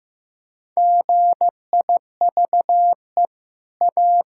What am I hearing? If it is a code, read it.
Morse code, heard as a keyed tone, about 15 wpm, 700 Hz. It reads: GIVE A
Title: Sentences from Top 1000 Words - Review Part 6 - 15wpm